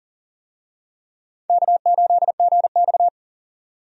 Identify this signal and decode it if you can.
Morse code, heard as a keyed tone, about 40 wpm, 700 Hz. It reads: K8GX
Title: USA Call Signs - 4 Characters - 40wpm